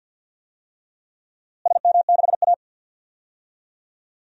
Morse code, heard as a keyed tone, about 53 wpm, 700 Hz. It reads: SM6A
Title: DXCC-100 Call Signs - 4 Characters - Repeated 3x - 50wpm